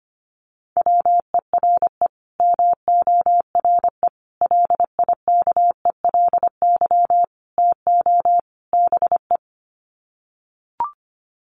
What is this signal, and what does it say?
Morse code, heard as a keyed tone, about 25 wpm, 700 Hz. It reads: WERE MORE LIKELY TO BE
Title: Sets of 5 Words Encore2 25wpm